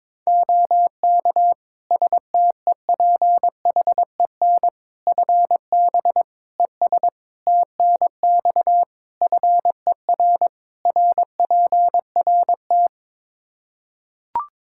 Morse code, heard as a keyed tone, about 21 wpm, 700 Hz. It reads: OK STEPHEN FB ES TNX FER RPRT
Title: Practice QSO - Exchange 2 - Info - 22wpm